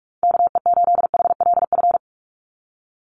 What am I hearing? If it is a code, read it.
Morse code, heard as a keyed tone, about 44 wpm, 700 Hz. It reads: KE8HLF
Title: USA Call Signs 45wpm